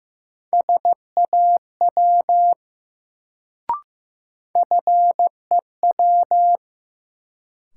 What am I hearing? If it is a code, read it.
Morse code, heard as a keyed tone, about 15 wpm, 700 Hz. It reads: SAW FEW
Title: Sentences from Top 300 Words - New Word Drill - 15wpm